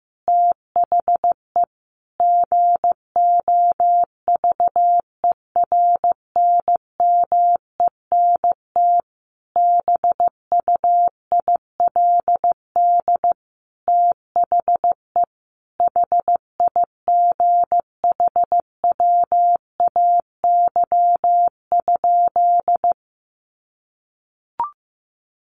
Morse code, heard as a keyed tone, about 15 wpm, 700 Hz. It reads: THE GOVERNMENT BUILD THE HIGHWAY?
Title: Sentences from Top 1400 Words 15wpm